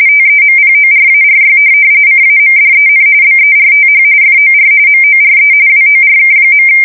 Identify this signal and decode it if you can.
RTTY, about 45.45 baud. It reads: TEMPORARY CAMDEN BERTH FOR MORE WORK.
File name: fsk170 / USS-New-Jersey